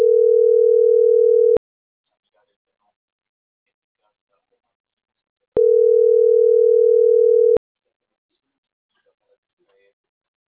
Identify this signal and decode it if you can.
Ringback tone (North America), 440 + 480 Hz.